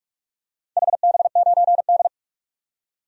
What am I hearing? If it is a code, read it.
Morse code, heard as a keyed tone, about 46 wpm, 700 Hz. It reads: HB9D
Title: DXCC-100 Call Signs - 4 Characters - 45wpm